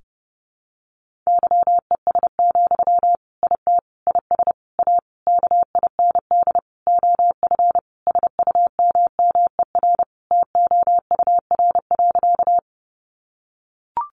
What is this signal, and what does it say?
Morse code, heard as a keyed tone, about 28 wpm, 700 Hz. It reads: YES, IT IS A KIND OF SUMMER TOUR.
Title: Sentences from Top 1400 Words 30wpm